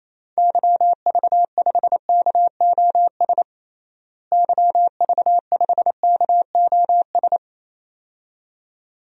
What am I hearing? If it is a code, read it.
Morse code, heard as a keyed tone, about 28 wpm, 700 Hz. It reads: YV5KOS YV5KOS
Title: DXCC-100 Call Signs - 6 Characters - Repeated 2x - 28wpm